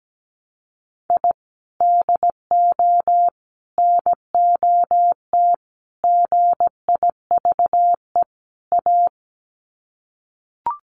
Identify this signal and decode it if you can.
Morse code, heard as a keyed tone, about 17 wpm, 700 Hz. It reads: I DO NOT GIVE A
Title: Sets of 5 Words Encore2 17wpm